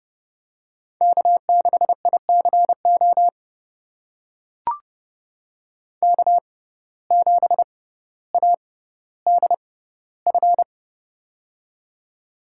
Morse code, heard as a keyed tone, about 30 wpm, spaced as for 15 wpm, 700 Hz. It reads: K6ICO K7ADF
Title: USA Call Signs - 5 Characters - 6x Character Spacing - 30wpm